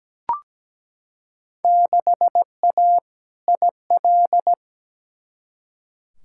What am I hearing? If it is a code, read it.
Morse code, heard as a keyed tone, about 17 wpm, 700 Hz. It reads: 6A IL
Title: Field Day Exchanges - 17wpm